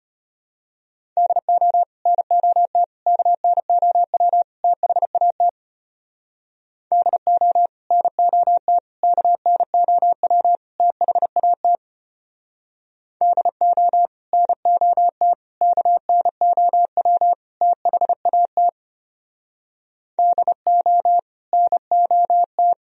Morse code, heard as a keyed tone, about 32 wpm, 700 Hz. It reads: DO NOT KNOW THAT DO NOT KNOW THAT DO NOT KNOW THAT DO NOT